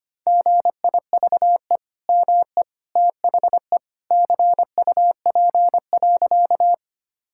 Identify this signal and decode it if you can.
Morse code, heard as a keyed tone, about 25 wpm, 700 Hz. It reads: GIVE ME THE CUP.
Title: Sentences from Top 1600 Words 25wpm